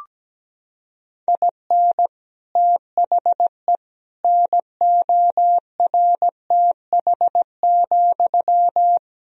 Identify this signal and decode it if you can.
Morse code, heard as a keyed tone, about 17 wpm, 700 Hz. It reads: IN THE NORTH,
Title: Sentences from Top 1000 Words - Review Part 1 - 17wpm